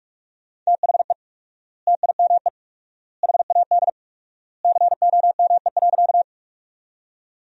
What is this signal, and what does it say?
Morse code, heard as a keyed tone, about 41 wpm, 700 Hz. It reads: THE TIME HAD COME.
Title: 4x Word-Spacing - Sentences from Top 100 Words - 45wpm